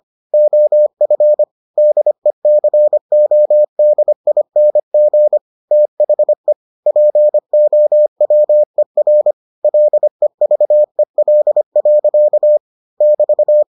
Morse code, heard as a keyed tone, about 25 wpm, 600 Hz. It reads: OF DECODING THE POWER LEVEL. <BT>